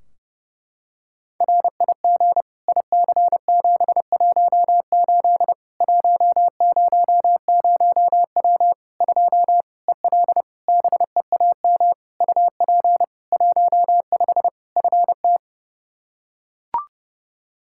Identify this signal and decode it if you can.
Morse code, heard as a keyed tone, about 30 wpm, 700 Hz. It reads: RIG IC718 100W 2 EL BEAM UP 15 FT
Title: Practice QSO - Exchange 2 - Info - 30wpm